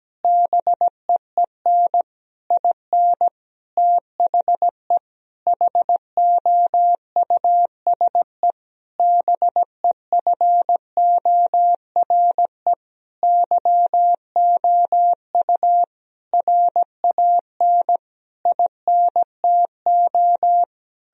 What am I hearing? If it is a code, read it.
Morse code, heard as a keyed tone, about 17 wpm, 700 Hz. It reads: BEEN IN THE HOUSE BEFORE YOU RAN INTO
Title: Sentences from Top 1000 Words - Review Part 3 - 17wpm